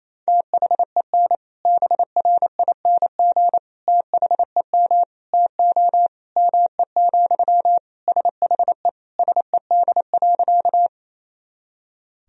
Morse code, heard as a keyed tone, about 28 wpm, 700 Hz. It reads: THEN BRING THEM TO ME, SHE SED.
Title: Sentences from Top 500 Words and Abbreviations 28wpm